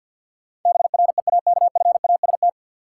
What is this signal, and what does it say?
Morse code, heard as a keyed tone, about 53 wpm, 700 Hz. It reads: BREAKFAST
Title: Sentences from Top 1500 Words - New Word Drill - 50wpm